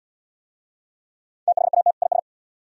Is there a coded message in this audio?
Yes: Morse code from 1.5 to 2.2 s, heard as a keyed tone, about 66 wpm, 690 Hz. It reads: THAT IS